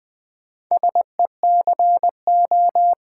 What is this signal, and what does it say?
Morse code, heard as a keyed tone, about 20 wpm, 700 Hz. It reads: SECO
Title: Build A Word - 2 Letters At A Time - 8-Letter Words - 20wpm